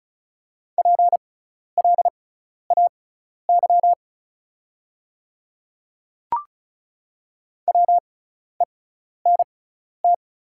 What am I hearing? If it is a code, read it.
Morse code, heard as a keyed tone, about 36 wpm, spaced as for 12 wpm, 690 Hz. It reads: PLAY WENT
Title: 6x Character Spacing - Top 300 Words - 35wpm